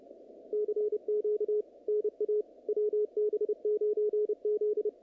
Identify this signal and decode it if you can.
Morse code, heard as a keyed tone, about 29 wpm, 440 Hz, with radio noise. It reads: CQ NA WB9Z
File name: contests / naqp-cw-k5tr-op-k5pi- -2200